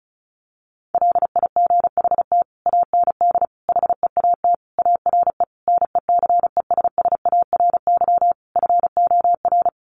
Morse code, heard as a keyed tone, about 34 wpm, 700 Hz. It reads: LIGHT AND HEAT ARE NECESSARY FOR
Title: Sentences from Top 1300 Words 35wpm